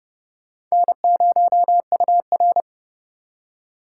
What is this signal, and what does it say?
Morse code, heard as a keyed tone, about 37 wpm, 700 Hz. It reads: N0UR
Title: Speed Racing - USA Call Signs 30wpm